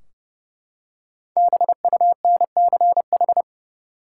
Morse code, heard as a keyed tone, about 30 wpm, 700 Hz. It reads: BUNCH
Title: Five Letter Words - 30wpm